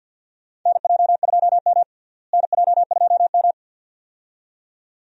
Morse code, heard as a keyed tone, about 48 wpm, 700 Hz. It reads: NJ2M NJ2M
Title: USA Call Signs - 4 Characters - Repeated 2x - 50wpm